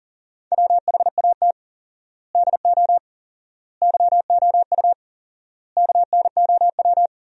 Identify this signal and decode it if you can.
Morse code, heard as a keyed tone, about 40 wpm, 700 Hz. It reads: WHAT DO YOU KNOW
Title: 4x Word-Spacing - Sentences from Top 100 Words - 40wpm